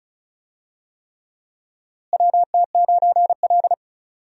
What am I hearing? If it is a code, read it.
Morse code, heard as a keyed tone, about 36 wpm, 700 Hz. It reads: WT9L
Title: USA Call Signs - 4 Characters - Repeated 3x - 35wpm